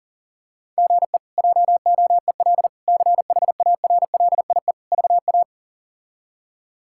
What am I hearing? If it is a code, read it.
Morse code, heard as a keyed tone, about 40 wpm, 700 Hz. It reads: GE JOEL CHARLIE VA
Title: K1USN SST QSO Practice - 40wpm